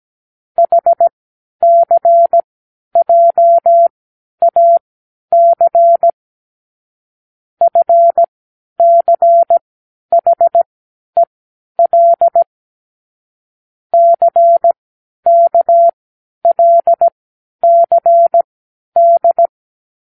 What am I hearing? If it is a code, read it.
Morse code, heard as a keyed tone, about 17 wpm, spaced as for 10 wpm, 680 Hz. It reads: HCJAC FCHEL CKLCD
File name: k7qo / Lesson18(A-L)